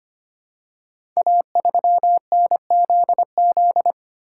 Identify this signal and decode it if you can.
Morse code, heard as a keyed tone, about 25 wpm, 700 Hz. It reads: A3NZZ
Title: Code Groups - 5 Letter:Numbers - 25wpm